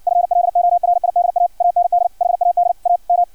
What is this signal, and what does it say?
Morse code, heard as a keyed tone, about 56 wpm, 700 Hz, with radio noise. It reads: FLORIDA AND HAD A M
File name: archived / CRSnet